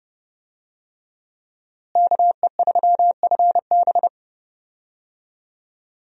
Morse code, heard as a keyed tone, about 30 wpm, 700 Hz. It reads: KE3FB
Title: USA Call Signs - 5 Characters - Repeated 3x - 30wpm